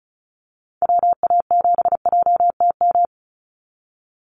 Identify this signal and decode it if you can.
Morse code, heard as a keyed tone, about 34 wpm, 700 Hz. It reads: WA7JTM
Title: NA SOTA Activator Callsigns 35wpm